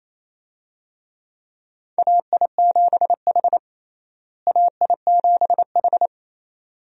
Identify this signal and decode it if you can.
Morse code, heard as a keyed tone, about 28 wpm, 700 Hz. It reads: AI7H AI7H
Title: USA Call Signs - 4 Characters - Repeated 2x - 28wpm